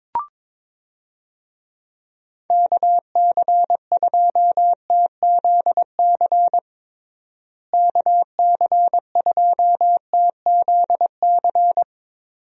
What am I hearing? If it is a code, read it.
Morse code, heard as a keyed tone, about 22 wpm, 700 Hz. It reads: KC2TZC KC2TZC
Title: USA Call Signs - 6 Characters - Repeated 2x - 22wpm